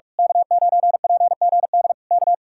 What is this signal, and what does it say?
Morse code, heard as a keyed tone, about 46 wpm, 700 Hz. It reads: K9PGD K
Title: Speed-Racing Common QSO Phrases - 45wpm